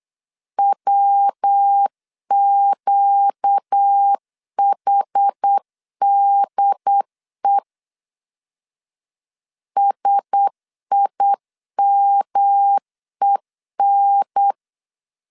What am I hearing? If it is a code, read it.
Morse code, heard as a keyed tone, about 8 wpm, 800 Hz. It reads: WQHDE SIMEN
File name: CW / DET / 1 6 1500C S B 8wpm 5142